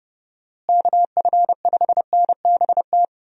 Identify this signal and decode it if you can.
Morse code, heard as a keyed tone, about 30 wpm, 700 Hz. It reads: KF5NBT
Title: USA Call Signs - 6 Characters - Repeated 3x - 30wpm